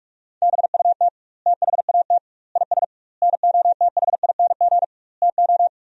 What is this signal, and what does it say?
Morse code, heard as a keyed tone, about 46 wpm, 700 Hz. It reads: BUT THAT IS NOTHING TO
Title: 2x Word-Spacing - Sentences from Top 500 Words - 45wpm